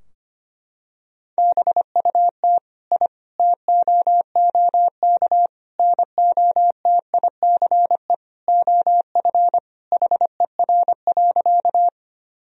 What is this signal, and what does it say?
Morse code, heard as a keyed tone, about 25 wpm, 700 Hz. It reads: BUT I TOOK NOTICE OF HER.
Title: Sentences from Top 1000 Words - Review Part 2 - 25wpm